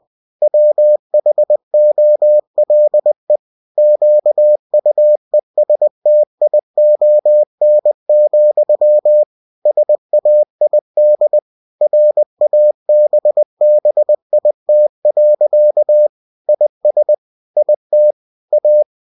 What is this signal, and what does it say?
Morse code, heard as a keyed tone, about 20 wpm, 600 Hz. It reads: WHOLE QUESTION, SAID RABBIT. IS IT A
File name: cw-books / winnie-the-pooh / winnie-20wpm-page-0069